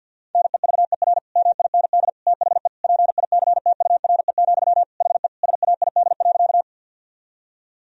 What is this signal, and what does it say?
Morse code, heard as a keyed tone, about 45 wpm, 700 Hz. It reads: NEVER MIND THE PICTURE, HE SAID.
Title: Sentences from Top 1000 Words - Review Part 1 - 50wpm